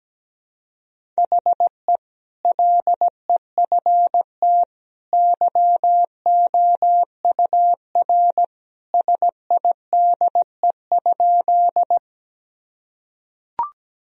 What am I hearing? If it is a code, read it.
Morse code, heard as a keyed tone, about 17 wpm, 700 Hz. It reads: HE LEFT YOUR SIDE?
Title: Sentences from Top 1000 Words - Review Part 1 - 17wpm